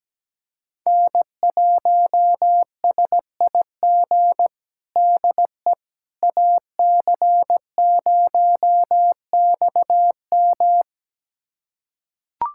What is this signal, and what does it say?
Morse code, heard as a keyed tone, about 17 wpm, 700 Hz. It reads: N1SIG DE AC0XM
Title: USA Callsign DE Callsign - 5 Characters - 17wpm